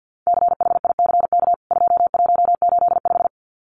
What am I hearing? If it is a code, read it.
Morse code, heard as a keyed tone, about 50 wpm, 700 Hz. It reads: CHICK 2185
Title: CWT Exchange 50wpm